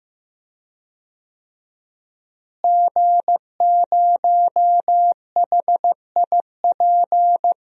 Morse code, heard as a keyed tone, about 15 wpm, 700 Hz. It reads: G0HIP
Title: DXCC-100 Call Signs - 5 Characters - Repeated 3x - 15wpm